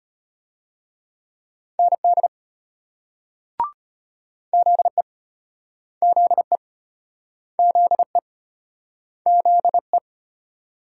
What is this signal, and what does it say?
Morse code, heard as a keyed tone, about 34 wpm, 700 Hz. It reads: ND ZE ZE ZE ZE